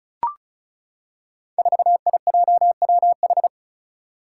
Morse code, heard as a keyed tone, about 36 wpm, 700 Hz. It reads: 4IJWH
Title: Five Challenging Letter-Number Combinations 35wpm